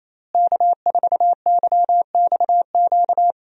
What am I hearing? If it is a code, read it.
Morse code, heard as a keyed tone, about 28 wpm, 700 Hz. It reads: K4YXQ
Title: USA Call Signs - 5 Characters - Repeated 3x - 28wpm